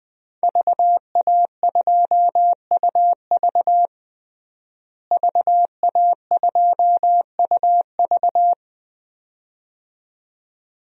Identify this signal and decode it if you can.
Morse code, heard as a keyed tone, about 20 wpm, 700 Hz. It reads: VA2UV VA2UV
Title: DXCC-100 Call Signs - 5 Characters - Repeated 2x - 20wpm